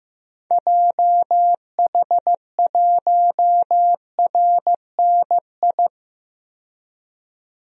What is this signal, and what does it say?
Morse code, heard as a keyed tone, about 15 wpm, 700 Hz. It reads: JH1RNI
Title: DXCC-100 Call Signs - 6 Characters - 15wpm